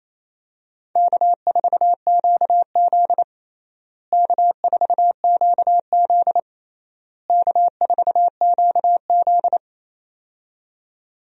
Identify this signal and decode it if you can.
Morse code, heard as a keyed tone, about 28 wpm, 700 Hz. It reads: K4QZ K4QZ K4QZ
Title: USA Call Signs - 4 Characters - Repeated 3x - 28wpm